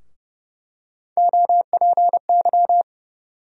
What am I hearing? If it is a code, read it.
Morse code, heard as a keyed tone, about 30 wpm, 700 Hz. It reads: OPY
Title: USA Call Sign Postfixes - 3 Letters 30wpm